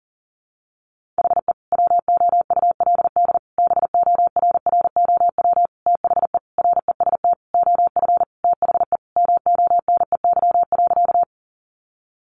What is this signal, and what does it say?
Morse code, heard as a keyed tone, about 40 wpm, 700 Hz. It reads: HE WOULD BORROW THE REST OF THE MONEY.
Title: Sentences from Top 1500 Words 40wpm